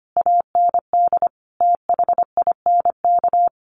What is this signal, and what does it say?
Morse code, heard as a keyed tone, about 25 wpm, 700 Hz. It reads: AND THINK
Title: Sentences from Top 1400 Words 25wpm